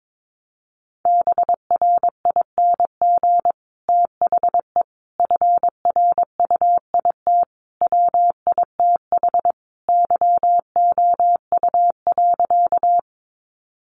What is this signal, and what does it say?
Morse code, heard as a keyed tone, about 22 wpm, 700 Hz. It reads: BRING THE FRUIT WITH YOU.
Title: Sentences from Top 1300 Words 22wpm